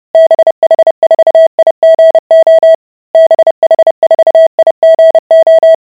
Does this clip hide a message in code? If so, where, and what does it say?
Yes: Morse code from 0.1 to 5.8 s, heard as a keyed tone, about 30 wpm, 650 Hz. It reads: BH4IGO BH4IGO